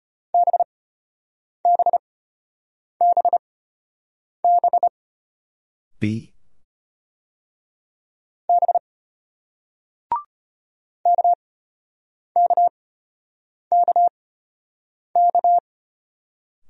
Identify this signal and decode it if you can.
Morse code, heard as a keyed tone, about 33 wpm, spaced as for 8 wpm, 700 Hz. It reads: BBBB B KKKK